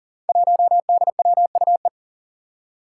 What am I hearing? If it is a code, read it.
Morse code, heard as a keyed tone, about 40 wpm, 700 Hz. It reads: 1DWUE
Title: Five Challenging Letter-Number Combinations 40wpm